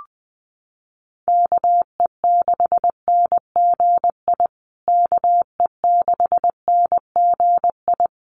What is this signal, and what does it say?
Morse code, heard as a keyed tone, about 20 wpm, 700 Hz. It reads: KE6NGI KE6NGI
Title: USA Call Signs Repeated 2x 20wpm